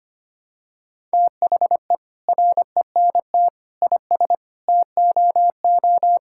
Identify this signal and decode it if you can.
Morse code, heard as a keyed tone, about 25 wpm, 700 Hz. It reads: THE RENT IS TOO
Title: Sentences from Top 1000 Words - Review Part 5 - 25wpm